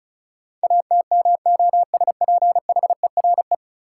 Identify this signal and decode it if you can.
Morse code, heard as a keyed tone, about 36 wpm, 700 Hz. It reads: ATMOSPHERE
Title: Sentences from Top 1600 Words - New Word Drill - 35wpm